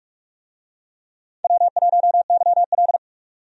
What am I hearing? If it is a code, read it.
Morse code, heard as a keyed tone, about 46 wpm, 700 Hz. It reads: W1YL
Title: USA Call Signs - 4 Characters - 45wpm